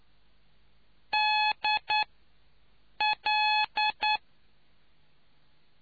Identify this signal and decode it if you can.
Morse code, heard as a keyed tone, about 9 wpm, 850 Hz. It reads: D L